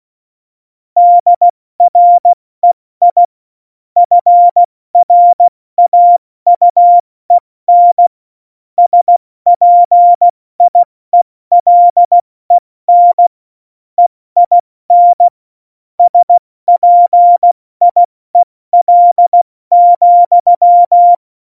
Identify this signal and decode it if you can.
Morse code, heard as a keyed tone, about 16 wpm, spaced as for 14 wpm, 700 Hz. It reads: DREI FRAUEN SPIELEN EIN SPIEL,
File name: Morsekurs / Witze / Witz 19. Januar 2021, 70 F80